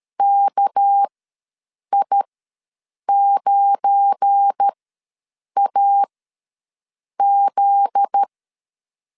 Morse code, heard as a keyed tone, about 13 wpm, spaced as for 7 wpm, 800 Hz. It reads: KI9AZ
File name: LEZIONI / 7 GF9Z / 7 GF9Z ws 12wpm 3321